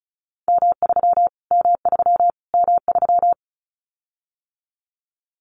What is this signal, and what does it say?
Morse code, heard as a keyed tone, about 34 wpm, 700 Hz. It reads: M3 M3 M3
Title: Callsign Prefixes for Top 25 DXCC - Repeated 3x - 35wpm